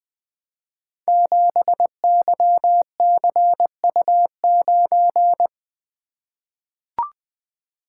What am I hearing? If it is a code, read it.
Morse code, heard as a keyed tone, about 20 wpm, 700 Hz. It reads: 7YCU9
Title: Code Groups - 5 Letter:Numbers - 20wpm